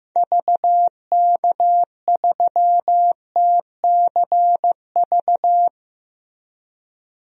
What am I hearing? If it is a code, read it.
Morse code, heard as a keyed tone, about 16 wpm, 700 Hz. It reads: VK3TCV
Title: DXCC-100 Call Signs - 6 Characters - Repeated 2x - 15wpm